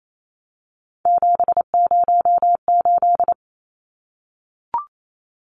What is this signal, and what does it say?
Morse code, heard as a keyed tone, about 28 wpm, 700 Hz. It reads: 708
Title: Numbers - 3-Digit 28wpm